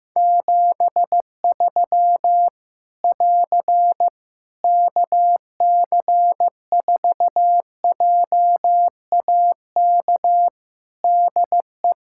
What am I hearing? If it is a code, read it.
Morse code, heard as a keyed tone, about 15 wpm, 700 Hz. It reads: 73 <AR> KC4JAK DE
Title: Practice QSO - Exchange 4 - Ending - 15wpm